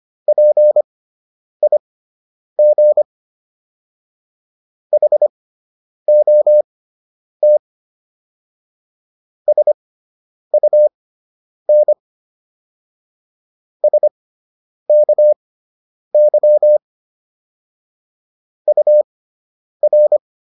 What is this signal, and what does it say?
Morse code, heard as a keyed tone, about 25 wpm, spaced as for 9 wpm, 600 Hz. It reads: PIG HOT SUN SKY UR